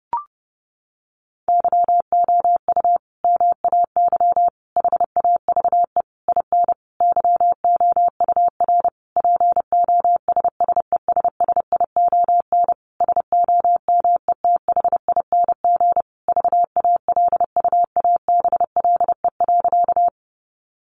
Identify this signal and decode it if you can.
Morse code, heard as a keyed tone, about 30 wpm, 700 Hz. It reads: YOU MAY HAVE IN YOUR POSSESSION SOMETHING VALUABLE.
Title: Sentences from Top 1400 Words 30wpm